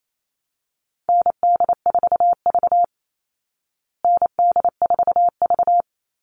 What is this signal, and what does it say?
Morse code, heard as a keyed tone, about 28 wpm, 700 Hz. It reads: ND4V ND4V
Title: Call Signs Repeated 3x - 28wpm